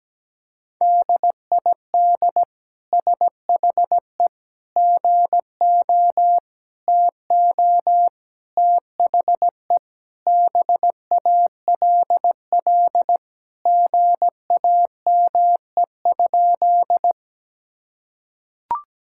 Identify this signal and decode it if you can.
Morse code, heard as a keyed tone, about 17 wpm, 700 Hz. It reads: DID SHE GO TO THE BALL GAME?
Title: Sentences from Top 1000 Words - Review Part 6 - 17wpm